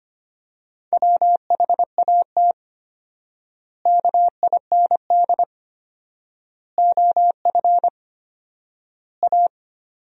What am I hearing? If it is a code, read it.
Morse code, heard as a keyed tone, about 25 wpm, 700 Hz. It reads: WHAT KIND OF A
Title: 4x Word-Spacing - Sentences from Top 400 Words - 25wpm